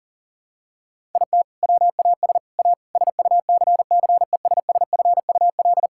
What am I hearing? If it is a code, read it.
Morse code, heard as a keyed tone, about 37 wpm, 700 Hz. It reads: IT WAS A SUCCESSFUL
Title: Sentences from Top 1000 Words - Review Part 6 - 40wpm